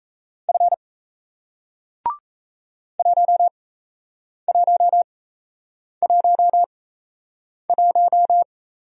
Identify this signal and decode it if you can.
Morse code, heard as a keyed tone, about 38 wpm, spaced as for 8 wpm, 700 Hz. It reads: F 1111